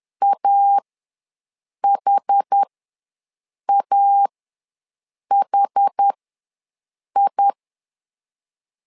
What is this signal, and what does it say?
Morse code, heard as a keyed tone, about 10 wpm, spaced as for 6 wpm, 800 Hz. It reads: AHAHI
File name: LEZIONI / 2 UHDN / 2 UH ws 10wpm 2029